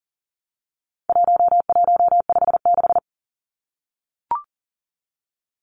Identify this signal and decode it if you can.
Morse code, heard as a keyed tone, about 39 wpm, 700 Hz. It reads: 1156
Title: Numbers - 4-Digit 40wpm